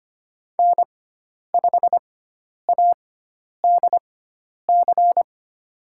Morse code, heard as a keyed tone, about 25 wpm, spaced as for 10 wpm, 700 Hz. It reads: N5ADC
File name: Lesson 16g - Letter c - Call Signs 25wpm